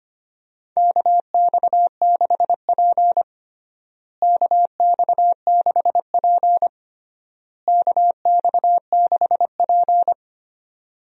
Morse code, heard as a keyed tone, about 25 wpm, 700 Hz. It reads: KX6P KX6P KX6P
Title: USA Call Signs - 4 Characters - Repeated 3x - 25wpm